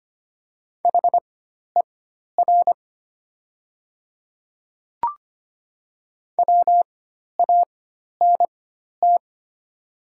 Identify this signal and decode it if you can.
Morse code, heard as a keyed tone, about 25 wpm, spaced as for 12 wpm, 690 Hz. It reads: HER WANT
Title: 4x Character Spacing - Top 200 Words - 25wpm